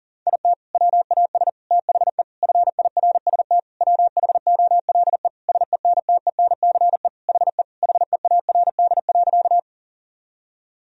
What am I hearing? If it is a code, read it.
Morse code, heard as a keyed tone, about 37 wpm, 700 Hz. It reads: IT WAS THE FIRST WHOLE SENTENCE HE HEARD.
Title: Sentences from Top 1000 Words - Review Part 1 - 40wpm